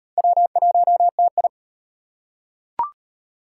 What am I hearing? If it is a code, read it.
Morse code, heard as a keyed tone, about 38 wpm, 700 Hz. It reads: W1TI